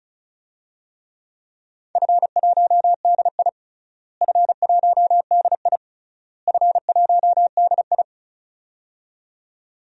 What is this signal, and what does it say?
Morse code, heard as a keyed tone, about 36 wpm, 700 Hz. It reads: F1DI F1DI F1DI
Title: DXCC-100 Call Signs - 4 Characters - Repeated 3x - 35wpm